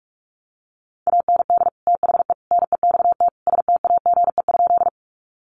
Morse code, heard as a keyed tone, about 44 wpm, 700 Hz. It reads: AND THE NEXT STAGE?
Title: Sentences from Top 1200 Words 45wpm